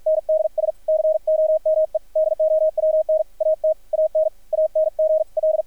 Morse code, heard as a keyed tone, about 50 wpm, 630 Hz, with radio noise. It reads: NDS COME DOWN AT AN ANGL
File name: archived / CRSnet